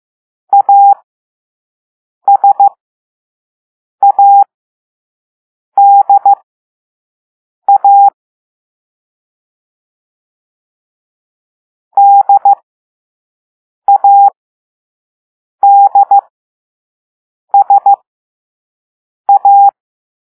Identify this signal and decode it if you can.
Morse code, heard as a keyed tone, about 15 wpm, spaced as for 5 wpm, 800 Hz. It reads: ASADA DADSA